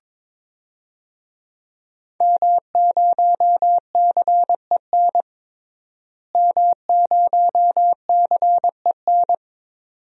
Morse code, heard as a keyed tone, about 23 wpm, 700 Hz. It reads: M0CEN M0CEN
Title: DXCC-100 Call Signs - 5 Characters - Repeated 2x - 22wpm